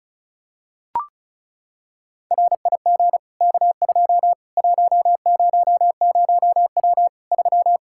Morse code, heard as a keyed tone, about 36 wpm, 700 Hz. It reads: RIG K2 100W 3
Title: K1USN SST QSO Practice 35wpm